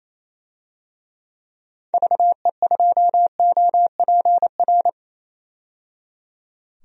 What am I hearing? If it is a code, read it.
Morse code, heard as a keyed tone, about 28 wpm, 700 Hz. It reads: VE2OPR
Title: DXCC-100 Call Signs - 6 Characters - 28wpm